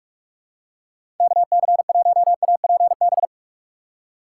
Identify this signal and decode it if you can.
Morse code, heard as a keyed tone, about 41 wpm, 700 Hz. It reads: KC1APB
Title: USA Call Signs - 6 Characters - 6x Character Spacing - 45wpm